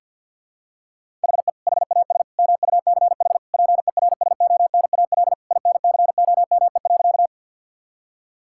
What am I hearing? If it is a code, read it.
Morse code, heard as a keyed tone, about 52 wpm, 700 Hz. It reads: HE HAS MUCH PERSONAL INCOME.
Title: Sentences from Top 1000 Words - Review Part 5 - 50wpm